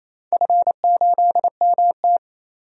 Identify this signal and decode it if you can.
Morse code, heard as a keyed tone, about 28 wpm, 700 Hz. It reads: F8MT
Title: DXCC-100 Call Signs - 4 Characters - 28wpm